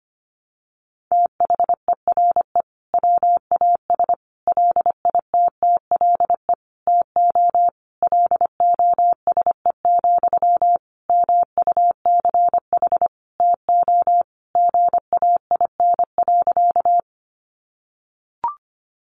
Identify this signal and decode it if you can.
Morse code, heard as a keyed tone, about 25 wpm, 700 Hz. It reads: THERE WAS LITTLE TO LOSE, MUCH TO GAIN.
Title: Sentences from Top 1100 Words 25wpm